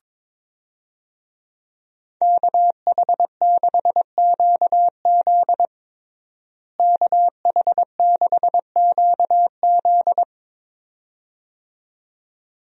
Morse code, heard as a keyed tone, about 22 wpm, 700 Hz. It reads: KH6QZ KH6QZ
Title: USA Call Signs - 5 Characters - Repeated 2x - 22wpm